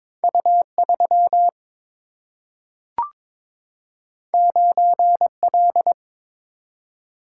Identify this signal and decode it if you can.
Morse code, heard as a keyed tone, about 22 wpm, 700 Hz. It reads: U3 9L
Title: Single-Letter Single-Number 22wpm